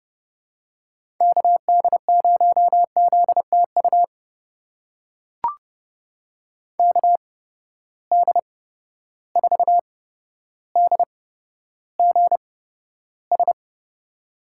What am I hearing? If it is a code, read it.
Morse code, heard as a keyed tone, about 31 wpm, spaced as for 13 wpm, 700 Hz. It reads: KD0ZTU KD4DGS
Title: USA Call Signs - 6 Characters - 8x Character Spacing - 30wpm